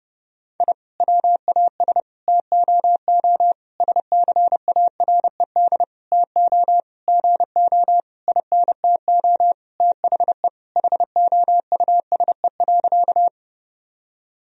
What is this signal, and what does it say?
Morse code, heard as a keyed tone, about 30 wpm, 700 Hz. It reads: I WAS TOO SCARED TO GO INTO THE HOUSE.
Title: Sentences from Top 1000 Words - Review Part 4 - 30wpm